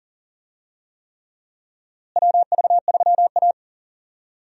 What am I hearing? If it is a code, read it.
Morse code, heard as a keyed tone, about 40 wpm, 700 Hz. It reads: WV3A
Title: USA Call Signs - 4 Characters - Repeated 3x - 40wpm